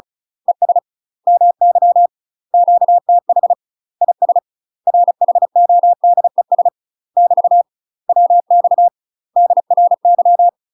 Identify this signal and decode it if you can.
Morse code, heard as a keyed tone, about 35 wpm, 700 Hz. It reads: ES MY QTH IS RHODES <BT> WX DRY
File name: UK MASTER QSOs TRAINING / 35WPM / UK MASTER QSOs 0035